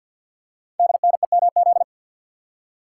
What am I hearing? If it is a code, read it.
Morse code, heard as a keyed tone, about 50 wpm, 700 Hz. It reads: DNEMZ
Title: Code Groups - 5 Letters - 50wpm